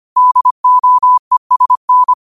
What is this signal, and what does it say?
Morse code, heard as a keyed tone, about 25 wpm, 1000 Hz. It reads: DOESN
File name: qotdfiles / 25wpm